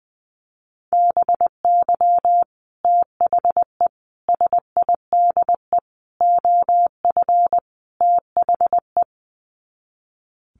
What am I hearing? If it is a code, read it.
Morse code, heard as a keyed tone, about 20 wpm, 700 Hz. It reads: BY THE SIDE OF THE
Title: Sets of 5 Words Encore2 20wpm